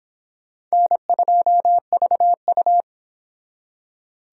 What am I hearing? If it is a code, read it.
Morse code, heard as a keyed tone, about 26 wpm, 700 Hz. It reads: N2VU